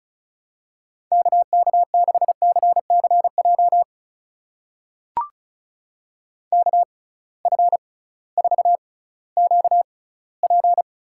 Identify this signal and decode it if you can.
Morse code, heard as a keyed tone, about 36 wpm, spaced as for 19 wpm, 700 Hz. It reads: KK6CCJ KF4QP
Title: USA Call Signs - 6 Characters - 6x Character Spacing - 35wpm